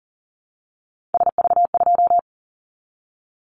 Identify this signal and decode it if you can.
Morse code, heard as a keyed tone, about 40 wpm, 700 Hz. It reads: SV2
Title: Callsign Prefixes for Top 50 DXCC - 40wpm